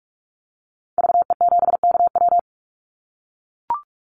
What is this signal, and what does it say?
Morse code, heard as a keyed tone, about 44 wpm, 700 Hz. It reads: VE7KW
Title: Call Signs - 45wpm